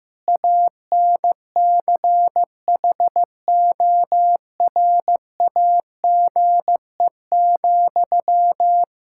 Morse code, heard as a keyed tone, about 15 wpm, 700 Hz. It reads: ANCHORAGE,
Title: Practice QSO - Exchange 1 - Beginning - 15wpm